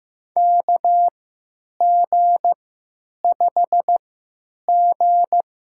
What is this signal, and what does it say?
Morse code, heard as a keyed tone, about 15 wpm, spaced as for 9 wpm, 700 Hz. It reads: KG5G
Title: USA Call Signs - 3x Character Spacing - 15wpm